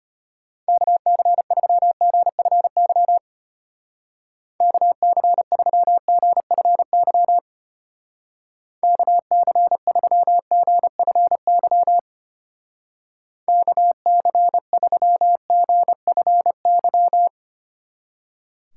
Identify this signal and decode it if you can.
Morse code, heard as a keyed tone, about 31 wpm, 700 Hz. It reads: KC3GFY KC3GFY KC3GFY KC3GFY
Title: Speed Racing - USA 2x3 Call Signs 25wpm